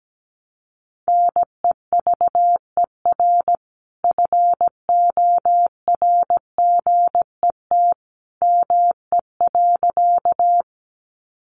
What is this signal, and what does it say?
Morse code, heard as a keyed tone, about 17 wpm, 700 Hz. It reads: NEVER FORGET ME.
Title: Sentences from Top 1200 Words 17wpm